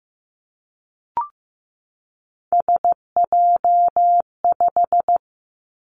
Morse code, heard as a keyed tone, about 15 wpm, 700 Hz. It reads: SJ5
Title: Three Challenging Letter-Number Combinations 15wpm